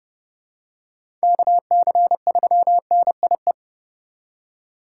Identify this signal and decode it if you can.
Morse code, heard as a keyed tone, about 31 wpm, 700 Hz. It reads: KC3NIE
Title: K1USN SST QSO Practice - 30wpm